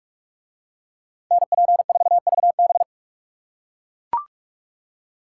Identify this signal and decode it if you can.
Morse code, heard as a keyed tone, about 43 wpm, 700 Hz. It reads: NP4VB